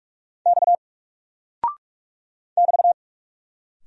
Morse code, heard as a keyed tone, about 44 wpm, 700 Hz. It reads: X <BT>
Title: Single Character (Letter:Number:Punctuation:Prosign) 45wpm